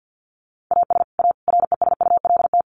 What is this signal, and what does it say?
Morse code, heard as a keyed tone, about 50 wpm, 700 Hz. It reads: AS A RESULT
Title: Sets of 3 Words 50wpm